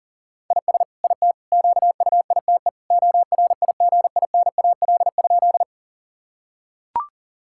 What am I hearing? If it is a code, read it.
Morse code, heard as a keyed tone, about 37 wpm, 700 Hz. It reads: IS IT QUITE ORIGINAL?
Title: Sentences from Top 1000 Words - Review Part 6 - 40wpm